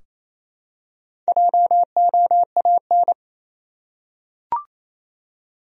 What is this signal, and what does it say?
Morse code, heard as a keyed tone, about 28 wpm, 700 Hz. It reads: JOAN
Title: USA Common Names - 28wpm